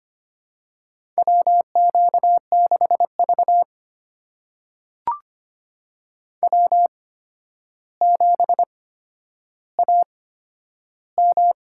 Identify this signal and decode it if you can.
Morse code, heard as a keyed tone, about 25 wpm, spaced as for 11 wpm, 700 Hz. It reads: WQ6V W7AM
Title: USA Call Signs - 4 Characters - 8x Character Spacing - 25wpm